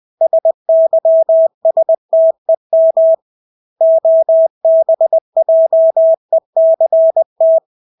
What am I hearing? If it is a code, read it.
Morse code, heard as a keyed tone, about 20 wpm, 650 Hz. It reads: SYSTEM OBJECT
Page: Engelske ord 20 wpm